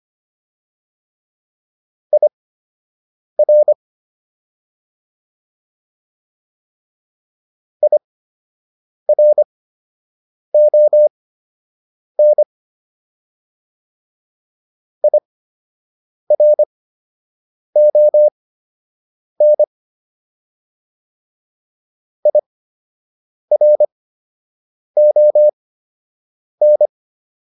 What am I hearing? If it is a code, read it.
Morse code, heard as a keyed tone, about 25 wpm, spaced as for 7 wpm, 600 Hz. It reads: IR IRON IRON IRON